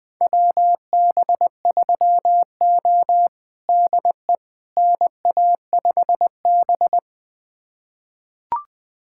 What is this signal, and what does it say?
Morse code, heard as a keyed tone, about 20 wpm, 700 Hz. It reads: WB3O DE NA5B
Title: USA Callsign DE Callsign - 4 Characters - 20wpm